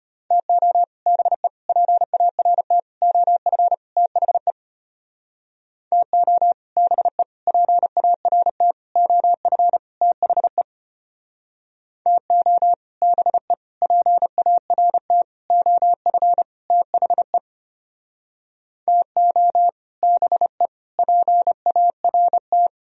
Morse code, heard as a keyed tone, about 32 wpm, 700 Hz. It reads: TO BE PART OF THE TO BE PART OF THE TO BE PART OF THE TO BE PART